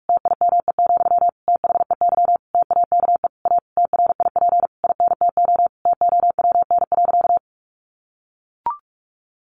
Morse code, heard as a keyed tone, about 44 wpm, 700 Hz. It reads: TIME, THEY TAKE A TRIP INTO TOWN.
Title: Sentences from Top 1300 Words 45wpm